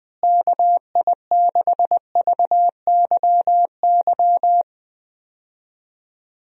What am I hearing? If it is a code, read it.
Morse code, heard as a keyed tone, about 21 wpm, 700 Hz. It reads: KI6VYY
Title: USA Call Signs - 6 Characters - Repeated 3x - 20wpm